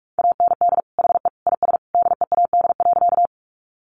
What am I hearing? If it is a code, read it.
Morse code, heard as a keyed tone, about 44 wpm, 700 Hz. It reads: AND HE IS DEAD.
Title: Sentences from Top 1500 Words 45wpm